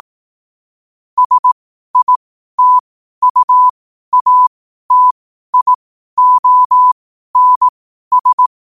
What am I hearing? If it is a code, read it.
Morse code, heard as a keyed tone, about 18 wpm, spaced as for 12 wpm, 1000 Hz. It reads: SITUATIONS
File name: qotdfiles / 12wpm